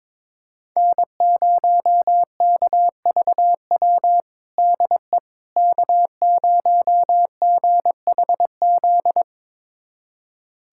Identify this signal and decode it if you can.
Morse code, heard as a keyed tone, about 21 wpm, 700 Hz. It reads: N0KVW DE K0GHZ
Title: USA Callsign DE Callsign - 5 Characters - 22wpm